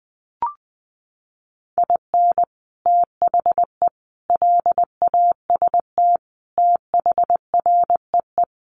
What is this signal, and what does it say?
Morse code, heard as a keyed tone, about 20 wpm, 700 Hz. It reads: IN THE LAST THREE
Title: Sets of 5 Words Encore2 20wpm